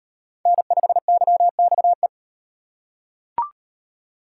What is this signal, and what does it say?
Morse code, heard as a keyed tone, about 35 wpm, 700 Hz. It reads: N5YXE